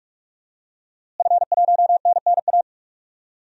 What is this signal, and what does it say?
Morse code, heard as a keyed tone, about 45 wpm, 700 Hz. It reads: F1NNA
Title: DXCC-100 Call Signs - 5 Characters - Repeated 3x - 45wpm